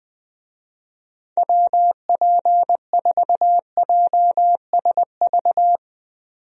Morse code, heard as a keyed tone, about 21 wpm, 700 Hz. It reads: WP4JSV
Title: USA Call Signs - 6 Characters - 20wpm